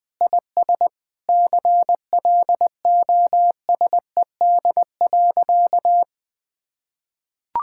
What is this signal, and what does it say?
Morse code, heard as a keyed tone, about 20 wpm, 700 Hz. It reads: IS CLOSED.
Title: Sentences with 7-Word Phrases - Spoken Beginning - 20wpm